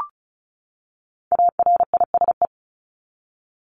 Morse code, heard as a keyed tone, about 34 wpm, 700 Hz. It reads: ARISE
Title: Top 1501-1750 Words 35wpm